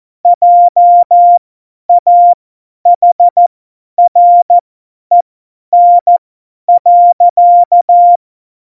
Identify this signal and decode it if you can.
Morse code, heard as a keyed tone, about 14 wpm, spaced as for 10 wpm, 700 Hz. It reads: JAHREN.
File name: Morsekurs / Witze / Witz 29. Dezember 2020, 50 F70